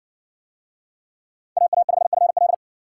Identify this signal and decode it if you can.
Morse code, heard as a keyed tone, about 61 wpm, 700 Hz. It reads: AA5FL